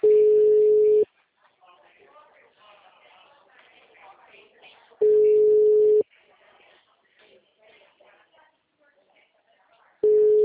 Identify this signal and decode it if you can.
Ringback tone (Europe, 425 Hz), with speech.
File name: out- -2021- - - .45421